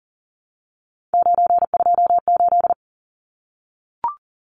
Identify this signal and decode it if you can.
Morse code, heard as a keyed tone, about 40 wpm, 700 Hz. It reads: 928
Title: Numbers - 3-Digit 40wpm